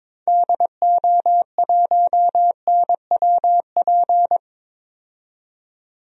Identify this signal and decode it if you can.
Morse code, heard as a keyed tone, about 21 wpm, 700 Hz. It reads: DO1NWP
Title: DXCC-100 Call Signs - 6 Characters - 22wpm